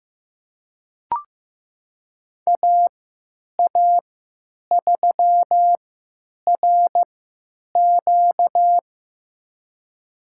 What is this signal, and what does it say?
Morse code, heard as a keyed tone, about 15 wpm, spaced as for 9 wpm, 700 Hz. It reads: AA3RQ
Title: USA Call Signs - 3x Character Spacing - 15wpm